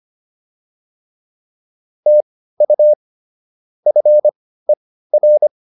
Morse code, heard as a keyed tone, about 25 wpm, spaced as for 15 wpm, 600 Hz. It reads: TU FER